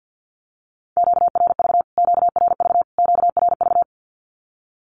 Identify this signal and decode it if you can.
Morse code, heard as a keyed tone, about 50 wpm, 700 Hz. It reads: QRV QRV QRV
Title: Q-Codes Repeated 3x 50wpm